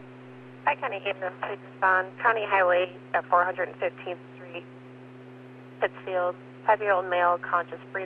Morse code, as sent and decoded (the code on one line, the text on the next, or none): none